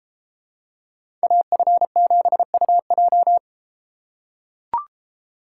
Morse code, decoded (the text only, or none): AF7UJ